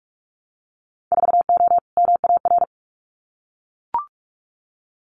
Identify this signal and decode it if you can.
Morse code, heard as a keyed tone, about 44 wpm, 700 Hz. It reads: 4O MAR
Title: Winter Field Day Exchanges Encore 45wpm